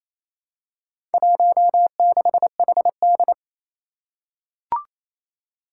Morse code, heard as a keyed tone, about 28 wpm, 700 Hz. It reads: 16HD
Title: Four Challenging Letter-Number Combinations 28wpm